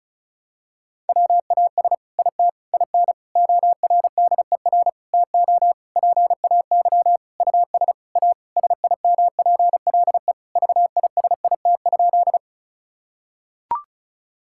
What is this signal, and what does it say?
Morse code, heard as a keyed tone, about 36 wpm, 700 Hz. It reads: WAS IT IN ORDER TO PAY US A SIMPLE VISIT?
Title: Sentences from Top 1000 Words - Review Part 6 - 35wpm